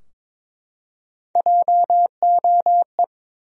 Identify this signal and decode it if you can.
Morse code, heard as a keyed tone, about 22 wpm, 700 Hz. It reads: JOE
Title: USA Common Names - Max 5 Letters - 22wpm